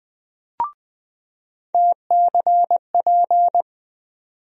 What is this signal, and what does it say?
Morse code, heard as a keyed tone, about 20 wpm, 700 Hz. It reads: TCP
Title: USA Call Sign Postfixes - 3 Letters 20wpm